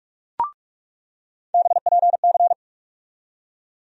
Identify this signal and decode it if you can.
Morse code, heard as a keyed tone, about 45 wpm, 700 Hz. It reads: BPC
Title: USA Call Sign Postfixes - 3 Letters 45wpm